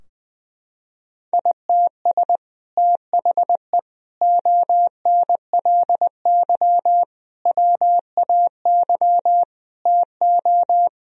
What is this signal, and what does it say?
Morse code, heard as a keyed tone, about 20 wpm, 700 Hz. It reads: ITS THE ONLY WAY TO
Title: Sentences from Top 1000 Words - Review Part 2 - 20wpm